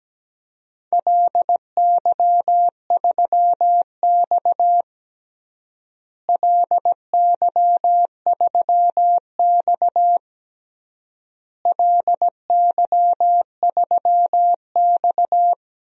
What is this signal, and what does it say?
Morse code, heard as a keyed tone, about 17 wpm, 700 Hz. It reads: LY3X LY3X LY3X
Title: DXCC-100 Call Signs - 4 Characters - Repeated 3x - 17wpm